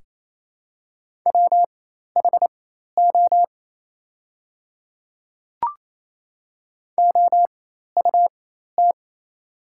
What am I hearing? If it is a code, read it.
Morse code, heard as a keyed tone, about 28 wpm, spaced as for 13 wpm, 700 Hz. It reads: WHO OUT
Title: 4x Character Spacing - Top 100 Words - 28wpm